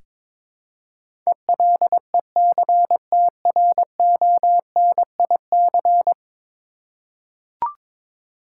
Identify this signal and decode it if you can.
Morse code, heard as a keyed tone, about 21 wpm, 700 Hz. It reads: ELECTRONIC
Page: morse-code-ninja-practice-22wpm